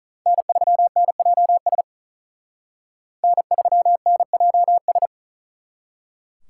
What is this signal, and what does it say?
Morse code, heard as a keyed tone, about 42 wpm, 700 Hz. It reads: N3NJS N3NJS
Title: Speed Racing - USA 1x3 Call Signs 35wpm